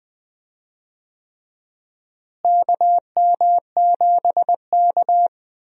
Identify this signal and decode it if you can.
Morse code, heard as a keyed tone, about 20 wpm, 700 Hz. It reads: KM7K
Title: USA Call Signs - 4 Characters - Repeated 3x - 20wpm